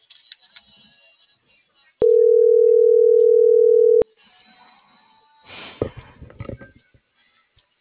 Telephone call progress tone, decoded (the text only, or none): ringback tone